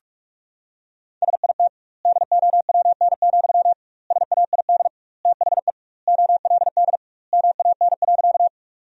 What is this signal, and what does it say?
Morse code, heard as a keyed tone, about 46 wpm, 700 Hz. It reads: SIT DOWN, SAID THE OLD MAN.
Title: 2x Word-Spacing - Sentences from Top 500 Words - 45wpm